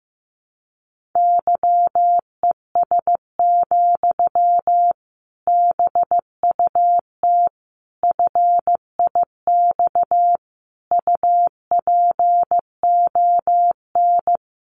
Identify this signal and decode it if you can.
Morse code, heard as a keyed tone, about 15 wpm, 700 Hz. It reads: YES, BUT FIX UPON
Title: Sentences from Top 1400 Words 15wpm